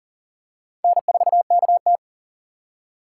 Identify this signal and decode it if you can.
Morse code, heard as a keyed tone, about 40 wpm, 700 Hz. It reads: N4KT
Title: USA Call Signs - 4 Characters - 2x Character Spacing - 40wpm